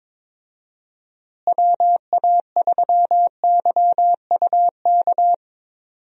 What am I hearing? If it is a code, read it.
Morse code, heard as a keyed tone, about 22 wpm, 700 Hz. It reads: WA3YUK
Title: USA Call Signs - 6 Characters - Repeated 2x - 22wpm